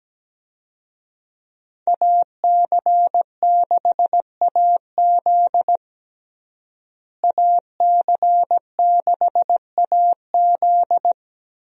Morse code, heard as a keyed tone, about 17 wpm, 700 Hz. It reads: AC6AZ AC6AZ
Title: USA Call Signs - 5 Characters - Repeated 3x - 17wpm